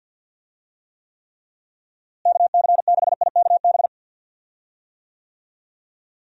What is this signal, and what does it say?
Morse code, heard as a keyed tone, about 52 wpm, 700 Hz. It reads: KC6IKB
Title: USA Call Signs - 6 Characters - Repeated 3x - 50wpm